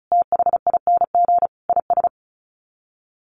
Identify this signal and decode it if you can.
Morse code, heard as a keyed tone, about 34 wpm, 700 Hz. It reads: THING IS
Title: Sets of 5 Words Encore 35wpm